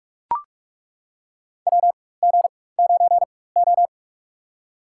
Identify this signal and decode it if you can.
Morse code, heard as a keyed tone, about 42 wpm, spaced as for 21 wpm, 700 Hz. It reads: WG9O
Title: USA Call Signs - 4 Characters - 4x Character Spacing - 45wpm